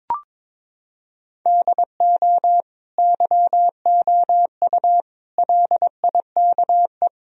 Morse code, heard as a keyed tone, about 21 wpm, 700 Hz. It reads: DO YOU LIKE
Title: Sentences from Top 1000 Words - Review Part 3 - 22wpm